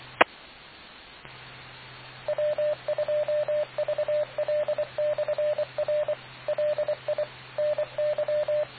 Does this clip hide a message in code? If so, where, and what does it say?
Yes: Morse code from 0.2 to 8.6 s, heard as a keyed tone, about 24 wpm, 600 Hz, with radio noise. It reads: E W2VL/R LI NY